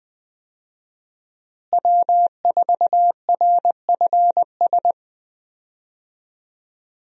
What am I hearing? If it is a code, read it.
Morse code, heard as a keyed tone, about 21 wpm, 700 Hz. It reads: W4RFS
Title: USA Call Signs - 5 Characters - Repeated 2x - 20wpm